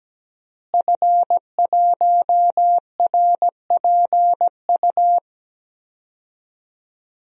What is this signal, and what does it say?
Morse code, heard as a keyed tone, about 17 wpm, 700 Hz. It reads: F1RPU
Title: DXCC-100 Call Signs - 5 Characters - Repeated 2x - 17wpm